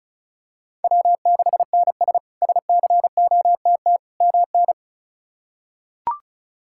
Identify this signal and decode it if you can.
Morse code, heard as a keyed tone, about 34 wpm, 700 Hz. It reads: W6NS SCOTT MN
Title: K1USN SST QSO Practice - 35wpm